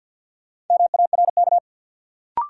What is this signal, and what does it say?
Morse code, heard as a keyed tone, about 52 wpm, 700 Hz. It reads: MARK
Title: Sentences from Top 300 Words - New Word Drill - 50wpm